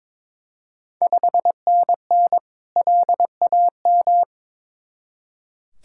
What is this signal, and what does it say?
Morse code, heard as a keyed tone, about 22 wpm, 690 Hz. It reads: 5NN LAM
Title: Ontario QSO Party Exchange 22wpm